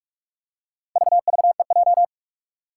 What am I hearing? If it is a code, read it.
Morse code, heard as a keyed tone, about 46 wpm, 700 Hz. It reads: VVEJ
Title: Four Challenging Letter-Number Combinations 45wpm